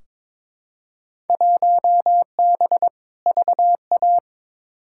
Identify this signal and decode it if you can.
Morse code, heard as a keyed tone, about 22 wpm, 700 Hz. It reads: 1B VA
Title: Field Day Exchanges - 22wpm